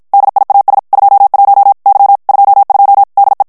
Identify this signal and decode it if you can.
Morse code, heard as a keyed tone, about 54 wpm, 800 Hz. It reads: DIAS 21 Y 22 DE
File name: ESPAGNOL TEXTE EN CLAIR 1 VITESSE 50 MOTS MINUTES